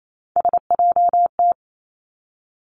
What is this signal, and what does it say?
Morse code, heard as a keyed tone, about 28 wpm, 700 Hz. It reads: SJT
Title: Three Challenging Letter-Number Combinations 28wpm